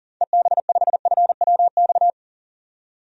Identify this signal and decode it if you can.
Morse code, heard as a keyed tone, about 39 wpm, 700 Hz. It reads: EB5FWX
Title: DXCC-100 Call Signs - 6 Characters - 40wpm